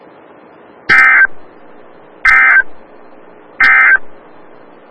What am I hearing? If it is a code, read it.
Emergency Alert System (SAME) tones, heard as harsh buzzing bursts. It is heard from 0.8 to 4.1 s.
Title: Des Moines, IA NOAA Weather Radio - WXL57